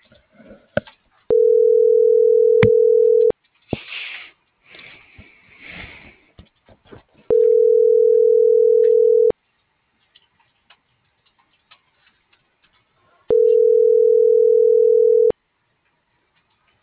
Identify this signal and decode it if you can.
Ringback tone (North America), 440 + 480 Hz.